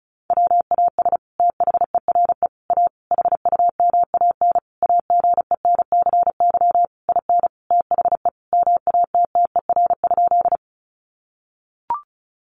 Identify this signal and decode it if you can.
Morse code, heard as a keyed tone, about 34 wpm, 700 Hz. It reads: WAS THERE A HUMAN AGENCY IN THE MATTER?
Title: Sentences from Top 1300 Words 35wpm